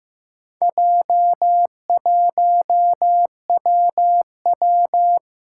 Morse code, heard as a keyed tone, about 15 wpm, 700 Hz. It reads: J1WW
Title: Four Challenging Letter-Number Combinations 15wpm